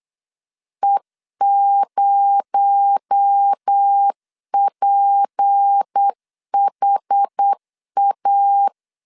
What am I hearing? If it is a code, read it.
Morse code, heard as a keyed tone, about 8 wpm, 800 Hz. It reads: E0PHA
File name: LEZIONI / 6 PKX0 / 6 PKX0 8wpm 3341